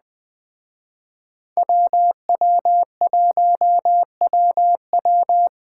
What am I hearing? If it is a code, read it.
Morse code, heard as a keyed tone, about 20 wpm, 700 Hz. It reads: WW1WW
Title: Speed Racing - USA 2x2 Call Signs 17wpm